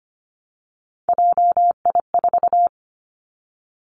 Morse code, heard as a keyed tone, about 25 wpm, 700 Hz. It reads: JI4
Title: Three Challenging Letter-Number Combinations 25wpm